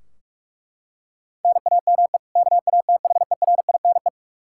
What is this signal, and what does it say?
Morse code, heard as a keyed tone, about 42 wpm, 700 Hz. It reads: NAME KATHERINE
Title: Practice QSO - Exchange 1 - Beginning - 45wpm